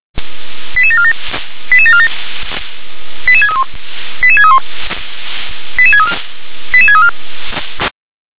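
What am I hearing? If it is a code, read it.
Five-tone selective call (ZVEI), heard as quick runs of five tones, tones of about 70 ms. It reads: 88747 88747 88631 88631 88634 88634
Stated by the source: ZVEI five-tone selective call received on an SDR.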